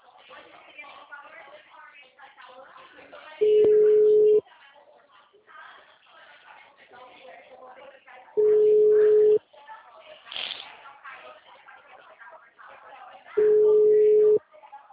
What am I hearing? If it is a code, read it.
Ringback tone (Europe, 425 Hz), with speech.
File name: out- -2021- - - .62985